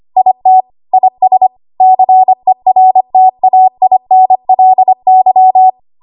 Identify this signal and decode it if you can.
Morse code, heard as a keyed tone, about 25 wpm, 750 Hz. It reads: IT IS CERTAINLY
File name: Archive / 25 WPM / 25WPM